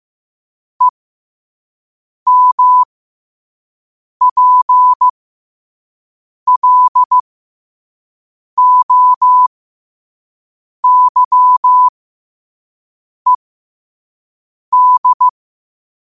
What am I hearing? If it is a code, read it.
Morse code, heard as a keyed tone, about 15 wpm, spaced as for 5 wpm, 1000 Hz. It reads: EMPLOYED
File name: 5wpm